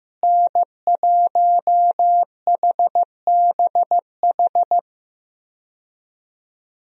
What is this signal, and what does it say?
Morse code, heard as a keyed tone, about 15 wpm, 700 Hz. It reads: N1HBH
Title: USA Call Signs - 5 Characters - Repeated 2x - 15wpm